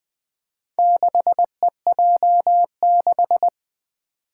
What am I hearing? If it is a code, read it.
Morse code, heard as a keyed tone, about 20 wpm, 700 Hz. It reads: 6EJ6
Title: Four Challenging Letter-Number Combinations 20wpm